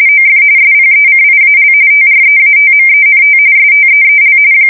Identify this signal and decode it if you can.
RTTY, about 45.45 baud. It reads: OF SPECIAL TREATMENT STEEL
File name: fsk170 / battleshipnj